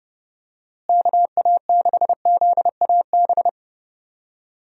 Morse code, heard as a keyed tone, about 30 wpm, 700 Hz. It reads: KA6ZAB
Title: USA Call Signs - 6 Characters - Repeated 2x - 30wpm